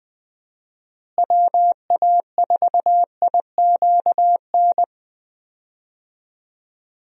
Morse code, heard as a keyed tone, about 20 wpm, 700 Hz. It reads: WA4IQN
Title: USA Call Signs - 6 Characters - Repeated 3x - 20wpm